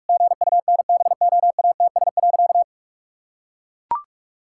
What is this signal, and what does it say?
Morse code, heard as a keyed tone, about 41 wpm, 700 Hz. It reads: GUNBOATS.
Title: Pangram Challenge v2 - 45wpm